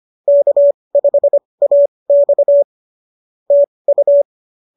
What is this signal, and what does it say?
Morse code, heard as a keyed tone, about 25 wpm, spaced as for 20 wpm, 590 Hz. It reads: K5AX TU